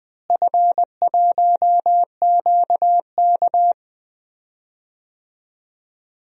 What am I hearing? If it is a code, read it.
Morse code, heard as a keyed tone, about 20 wpm, 700 Hz. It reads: F1QK
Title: DXCC-100 Call Signs - 4 Characters - Repeated 2x - 20wpm